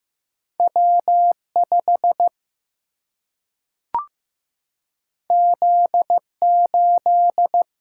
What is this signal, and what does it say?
Morse code, heard as a keyed tone, about 15 wpm, 700 Hz. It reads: W5 Z8
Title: Single-Letter Single-Number 15wpm